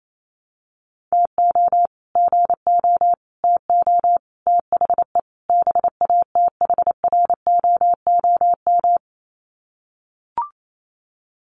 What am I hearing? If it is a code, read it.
Morse code, heard as a keyed tone, about 28 wpm, 690 Hz. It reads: TO GO TO THE BATHROOM
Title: Sets of 5 Words Encore 28wpm